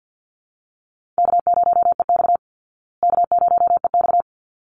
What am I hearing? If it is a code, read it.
Morse code, heard as a keyed tone, about 50 wpm, 700 Hz. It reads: K0EX K0EX
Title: Easy Call Signs Repeated 2x - 50wpm